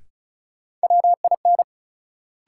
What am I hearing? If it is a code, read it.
Morse code, heard as a keyed tone, about 34 wpm, 700 Hz. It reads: WIN
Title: Sentences from Top 1500 Words - New Word Drill Alt - 35wpm